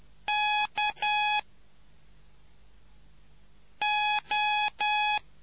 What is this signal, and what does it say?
Morse code, heard as a keyed tone, about 10 wpm, 840 Hz. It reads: K O